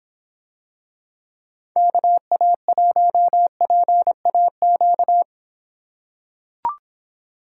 Morse code, heard as a keyed tone, about 24 wpm, 700 Hz. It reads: KA1PAQ